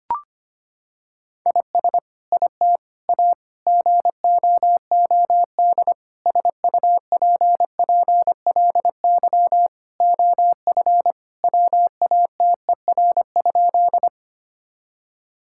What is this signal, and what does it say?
Morse code, heard as a keyed tone, about 25 wpm, 700 Hz. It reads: IS IT A GOOD SUPPLY OF WATER?
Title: Sentences from Top 1000 Words - Review Part 5 - 25wpm